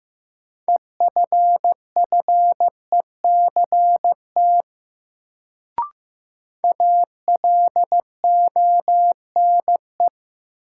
Morse code, heard as a keyed tone, about 15 wpm, 700 Hz. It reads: EFFECT ALONE
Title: Sentences from Top 1000 Words - New Word Drill - 15wpm